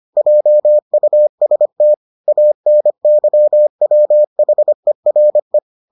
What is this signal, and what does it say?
Morse code, heard as a keyed tone, about 25 wpm, 600 Hz. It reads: JUST ANYWHERE